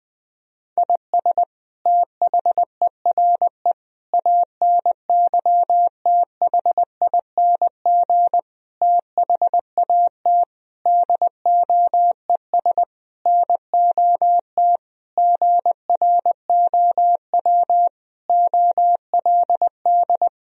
Morse code, heard as a keyed tone, about 20 wpm, 700 Hz. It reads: IS THERE ANYTHING THAT DOES NOT GROW OLD
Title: Sentences from Top 1000 Words - Review Part 1 - 20wpm